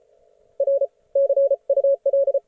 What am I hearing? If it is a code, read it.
Morse code, heard as a keyed tone, about 32 wpm, 530 Hz. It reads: R CUL